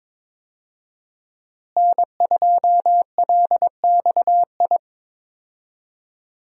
Morse code, heard as a keyed tone, about 22 wpm, 700 Hz. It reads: N2LXI
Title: USA Call Signs - 5 Characters - Repeated 3x - 22wpm